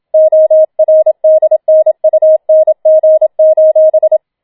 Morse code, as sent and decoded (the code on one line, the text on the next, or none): --- .-. -.. -. ..- -. --. ---...
ORDNUNG: